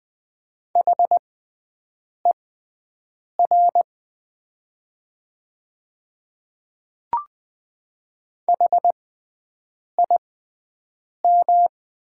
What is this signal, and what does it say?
Morse code, heard as a keyed tone, about 20 wpm, spaced as for 7 wpm, 690 Hz. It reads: HER HIM
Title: 6x Character Spacing - Top 200 Words - 20wpm